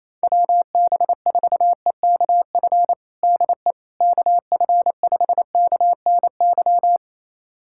Morse code, heard as a keyed tone, about 28 wpm, 700 Hz. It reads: WB4EKF DE KF5KNY
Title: USA Callsign DE Callsign - 6 Characters - 28wpm